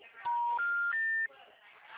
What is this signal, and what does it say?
Special information tone, with speech.